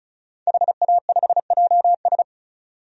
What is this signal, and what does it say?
Morse code, heard as a keyed tone, about 34 wpm, 700 Hz. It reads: HA5JS
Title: DXCC-100 Call Signs - 5 Characters - 35wpm